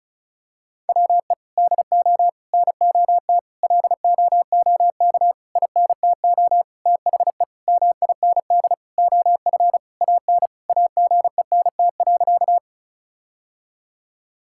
Morse code, heard as a keyed tone, about 36 wpm, 700 Hz. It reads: WE DO NOT LOOK INTO THE MIND OF AN AGENT.
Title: Sentences from Top 1600 Words 35wpm